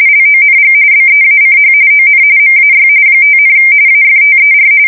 RTTY, about 45.45 baud. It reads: LONG BEACH ON MAY 5, AFTER
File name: fsk170 / USS-New-Jersey